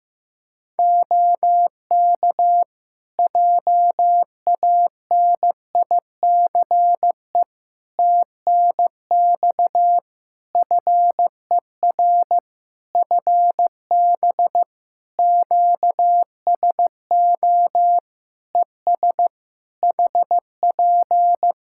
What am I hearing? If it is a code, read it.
Morse code, heard as a keyed tone, about 15 wpm, 700 Hz. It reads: OK JANICE TNX FER FB QSO ES HP
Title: K1USN SST QSO Practice 15wpm